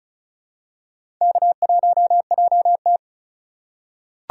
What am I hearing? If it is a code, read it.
Morse code, heard as a keyed tone, about 36 wpm, 700 Hz. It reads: K1JT
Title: USA Call Signs - 4 Characters - 3x Character Spacing - 35wpm